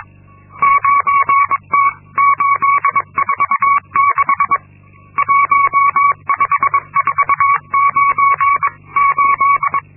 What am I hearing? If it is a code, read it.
Morse code, heard as a keyed tone, about 22 wpm, 1120 Hz. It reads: 9T846 15498
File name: Linksmakalnis-2